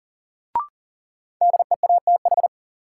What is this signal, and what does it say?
Morse code, heard as a keyed tone, about 40 wpm, 700 Hz. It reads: DEATH
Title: Sentences from Top 1100 Words - New Word Drill Alt - 40wpm